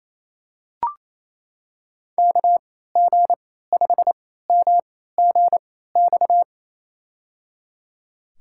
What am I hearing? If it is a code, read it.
Morse code, heard as a keyed tone, about 28 wpm, spaced as for 16 wpm, 700 Hz. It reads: KG5MGX
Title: USA Call Signs - 3x Character Spacing - 28wpm